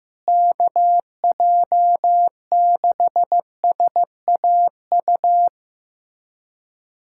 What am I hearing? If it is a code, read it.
Morse code, heard as a keyed tone, about 15 wpm, 700 Hz. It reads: KJ6SAU
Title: USA Callsign DE Callsign 15wpm